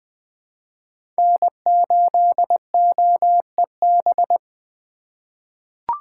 Morse code, heard as a keyed tone, about 21 wpm, 700 Hz. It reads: N8OEB
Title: USA Call Signs - 5 Characters - 4x Character Spacing - 20wpm